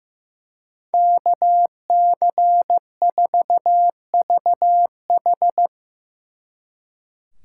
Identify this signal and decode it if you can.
Morse code, heard as a keyed tone, about 16 wpm, 700 Hz. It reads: KC4VH
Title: Speed Racing - USA 2x2 Call Signs 15wpm